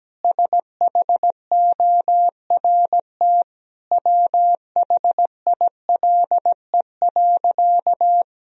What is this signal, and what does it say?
Morse code, heard as a keyed tone, about 17 wpm, 700 Hz. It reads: SHORT WHILE.
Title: Sentences from Top 1000 Words - Review Part 2 - 17wpm